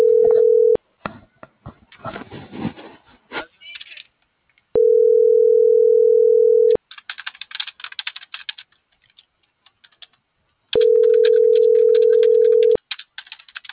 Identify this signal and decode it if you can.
Ringback tone (North America), 440 + 480 Hz.